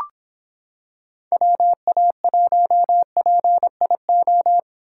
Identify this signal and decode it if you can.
Morse code, heard as a keyed tone, about 26 wpm, 700 Hz. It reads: WA1PIO